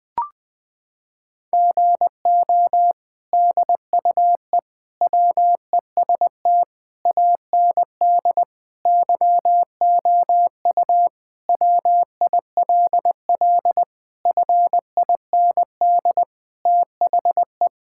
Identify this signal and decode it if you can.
Morse code, heard as a keyed tone, about 20 wpm, 700 Hz. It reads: GO DUE WEST AND YOU WILL FIND THE
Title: Sentences from Top 1000 Words - Review Part 6 - 20wpm